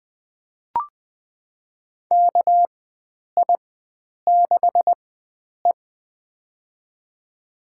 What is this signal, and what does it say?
Morse code, heard as a keyed tone, about 20 wpm, spaced as for 9 wpm, 700 Hz. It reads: KI6E
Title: USA Call Signs - 4 Characters - 4x Character Spacing - 20wpm